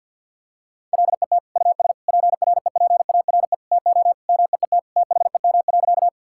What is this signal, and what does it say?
Morse code, heard as a keyed tone, about 50 wpm, 700 Hz. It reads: LET US PREPARE TO MEET THEM.
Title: Sentences from Top 1000 Words - Review Part 6 - 50wpm